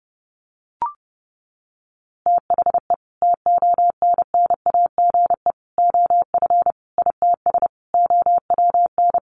Morse code, heard as a keyed tone, about 30 wpm, 700 Hz. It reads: THE TONNAGE OF ITS OWN
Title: A Sound Of Thunder 30wpm